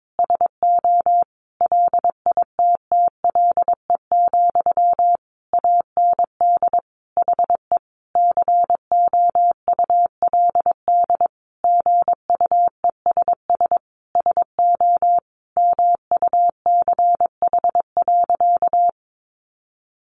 Morse code, heard as a keyed tone, about 22 wpm, 700 Hz. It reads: SO LITTLE, AND HE COULD GUESS SO MUCH.
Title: Sentences from Top 1500 Words 22wpm